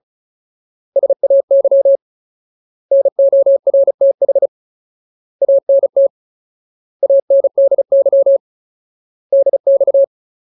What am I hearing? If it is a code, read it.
Morse code, heard as a keyed tone, about 34 wpm, 550 Hz. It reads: SAY NORTH ANT ANDY DX
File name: Ultimate Word List / 35WPM Ultimate